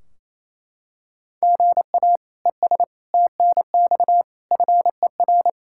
Morse code, heard as a keyed tone, about 28 wpm, 700 Hz. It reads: GA ES TNX FER
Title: Practice QSO - Exchange 1 - Beginning - 28wpm